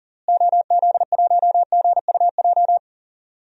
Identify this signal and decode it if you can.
Morse code, heard as a keyed tone, about 40 wpm, 700 Hz. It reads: OZ1GUJ
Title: DXCC-100 Call Signs - 6 Characters - Repeated 3x - 40wpm